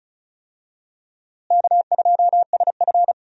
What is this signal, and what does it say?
Morse code, heard as a keyed tone, about 36 wpm, 700 Hz. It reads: K2SF
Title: USA Call Signs - 4 Characters - Repeated 3x - 35wpm